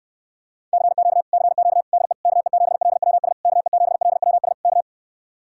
Morse code, heard as a keyed tone, about 58 wpm, 700 Hz. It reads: CQ CQ DE KE7LWS KE7LWS K